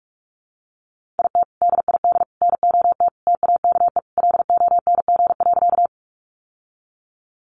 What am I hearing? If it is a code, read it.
Morse code, heard as a keyed tone, about 44 wpm, 700 Hz. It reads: IT DID NOT TAKE LONG.
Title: Sentences from Top 600 Words 45wpm